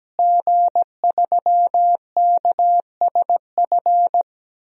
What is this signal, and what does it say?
Morse code, heard as a keyed tone, about 17 wpm, 700 Hz. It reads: G3KSF
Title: DXCC-100 Call Signs - 5 Characters - Repeated 3x - 17wpm